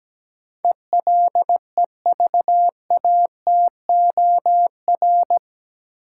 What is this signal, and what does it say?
Morse code, heard as a keyed tone, about 17 wpm, 700 Hz. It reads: ELEVATOR
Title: Sentences from Top 1600 Words - New Word Drill Alt - 17wpm